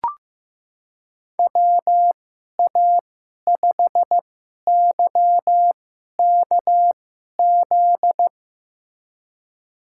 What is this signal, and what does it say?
Morse code, heard as a keyed tone, about 15 wpm, spaced as for 11 wpm, 700 Hz. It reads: WA5YKZ
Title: USA Call Signs - 6 Characters - 2x Character Spacing - 15wpm